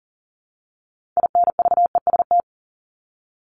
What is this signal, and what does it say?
Morse code, heard as a keyed tone, about 40 wpm, 700 Hz. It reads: INVEST
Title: Top 1001-1250 Words 40wpm